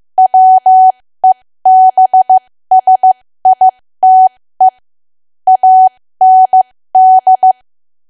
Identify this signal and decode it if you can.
Morse code, heard as a keyed tone, about 15 wpm, spaced as for 13 wpm, 750 Hz. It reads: WEBSITE AND
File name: cwa / W1AW-Level2 / 13WPM